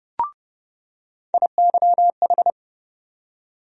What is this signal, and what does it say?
Morse code, heard as a keyed tone, about 30 wpm, 700 Hz. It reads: IYH
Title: USA Call Sign Postfixes - 3 Letters 30wpm